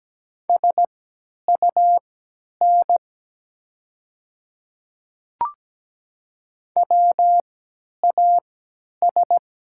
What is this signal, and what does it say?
Morse code, heard as a keyed tone, about 17 wpm, spaced as for 10 wpm, 700 Hz. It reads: SUN WAS
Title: 3x Character Spacing - Top 300 Words - 17wpm